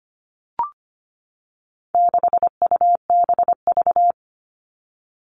Morse code, heard as a keyed tone, about 25 wpm, 700 Hz. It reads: 6UBV
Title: All Cut Numbers - 4-Digit 25wpm